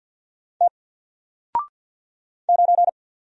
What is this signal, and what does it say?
Morse code, heard as a keyed tone, about 48 wpm, 700 Hz. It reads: T 9
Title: Single Letter-Number Alt 50wpm